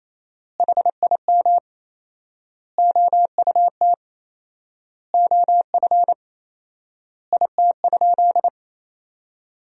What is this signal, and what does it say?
Morse code, heard as a keyed tone, about 28 wpm, 700 Hz. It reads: HIM OUT OF IT?
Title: 4x Word-Spacing - Sentences from Top 300 Words - 28wpm